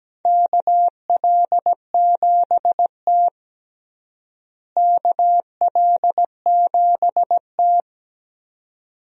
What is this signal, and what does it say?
Morse code, heard as a keyed tone, about 17 wpm, 700 Hz. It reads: KL7T KL7T
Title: USA Call Signs - 4 Characters - Repeated 2x - 17wpm